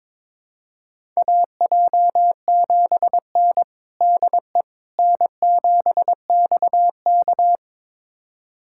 Morse code, heard as a keyed tone, about 21 wpm, 700 Hz. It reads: AJ7N DE N7XK
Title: USA Callsign DE Callsign - 4 Characters - 22wpm